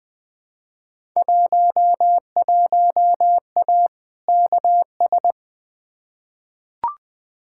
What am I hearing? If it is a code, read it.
Morse code, heard as a keyed tone, about 20 wpm, 700 Hz. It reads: 11A KS
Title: Field Day Exchanges Encore - 20wpm